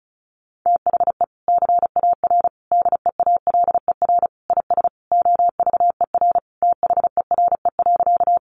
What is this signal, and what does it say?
Morse code, heard as a keyed tone, about 34 wpm, 700 Hz. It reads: THE CAR DEALER IS OVER THERE.
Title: Sentences from Top 1100 Words 35wpm